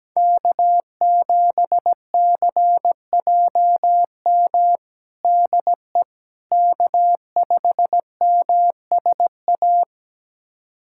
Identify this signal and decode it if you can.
Morse code, heard as a keyed tone, about 17 wpm, 700 Hz. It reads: K7CJM DE K5MSA
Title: USA Callsign DE Callsign - 5 Characters - 17wpm